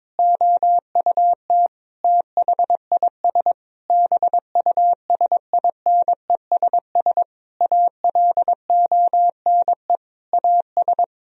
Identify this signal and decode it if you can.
Morse code, heard as a keyed tone, about 22 wpm, 700 Hz. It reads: OUT THIS BUSINESS ALONE AS
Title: Sentences from Top 1000 Words - Review Part 2 - 22wpm